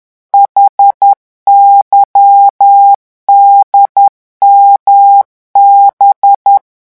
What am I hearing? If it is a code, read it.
Morse code, heard as a keyed tone, about 10 wpm, 800 Hz. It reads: HYDMB
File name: morse mp3 / revision5-vitesse 10 mots